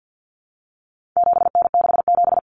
Morse code, heard as a keyed tone, about 50 wpm, 700 Hz. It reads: 7N67
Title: Common Cut Numbers - 4-Digit 50wpm